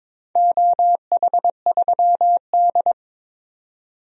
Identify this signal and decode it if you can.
Morse code, heard as a keyed tone, about 22 wpm, 700 Hz. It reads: OH3D
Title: DXCC-100 Call Signs - 4 Characters - 22wpm